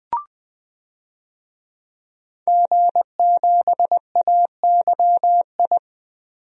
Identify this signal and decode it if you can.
Morse code, heard as a keyed tone, about 20 wpm, 700 Hz. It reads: G7AYI
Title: DXCC-100 Call Signs - 5 Characters - 20wpm